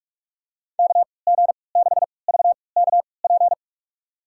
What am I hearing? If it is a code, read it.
Morse code, heard as a keyed tone, about 43 wpm, spaced as for 25 wpm, 700 Hz. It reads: KG6VKP
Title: USA Call Signs - 6 Characters - 3x Character Spacing - 45wpm